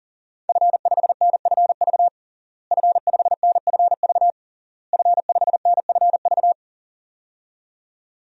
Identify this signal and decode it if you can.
Morse code, heard as a keyed tone, about 40 wpm, 700 Hz. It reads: F5NFV F5NFV F5NFV
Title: DXCC-100 Call Signs - 5 Characters - Repeated 3x - 40wpm